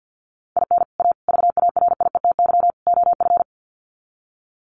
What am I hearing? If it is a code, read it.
Morse code, heard as a keyed tone, about 50 wpm, 700 Hz. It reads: IN A VARIETY OF
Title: Sets of 4 Words 50wpm